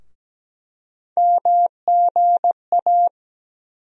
Morse code, heard as a keyed tone, about 17 wpm, 700 Hz. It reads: MGA
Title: USA Call Sign Postfixes - 3 Letters 17wpm